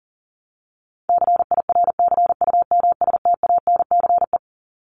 Morse code, heard as a keyed tone, about 40 wpm, 700 Hz. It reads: CIRCUMSTANCE
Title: ES:S Suffix - Top 100 Words with Base Words 40wpm